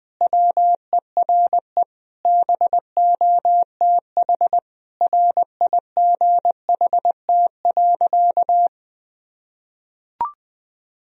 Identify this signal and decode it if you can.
Morse code, heard as a keyed tone, about 20 wpm, 700 Hz. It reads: WERE BOTH RIGHT.
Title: Sentences from Top 1000 Words - Review Part 1 - 20wpm